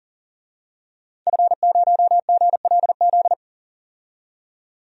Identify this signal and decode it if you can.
Morse code, heard as a keyed tone, about 40 wpm, 700 Hz. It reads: F0GLZ
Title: DXCC-100 Call Signs - 5 Characters - 40wpm